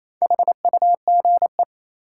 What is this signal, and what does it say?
Morse code, heard as a keyed tone, about 28 wpm, 700 Hz. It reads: HUGE
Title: Sentences from Top 1100 Words - New Word Drill Alt - 28wpm